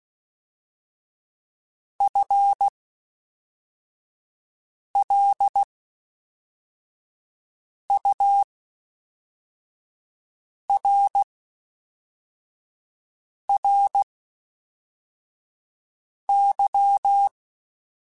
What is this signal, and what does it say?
Morse code, heard as a keyed tone, about 16 wpm, spaced as for 4 wpm, 800 Hz. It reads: FLURRY
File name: Spaced / FL Runs sp mp3s / FL Runs sp @5